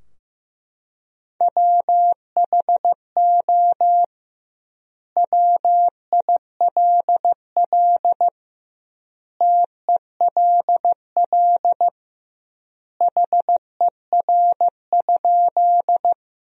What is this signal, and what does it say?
Morse code, heard as a keyed tone, about 15 wpm, 700 Hz. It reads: WHO WILL TELL HER?
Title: 2x Word-Spacing - Sentences from Top 200 Words - 15wpm